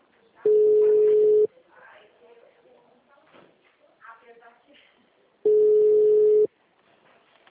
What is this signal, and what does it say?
Ringback tone (Europe, 425 Hz), with speech.